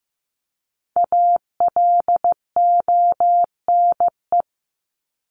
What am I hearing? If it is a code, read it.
Morse code, heard as a keyed tone, about 15 wpm, 700 Hz. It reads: ALONE
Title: Top 751-1000 Words 15wpm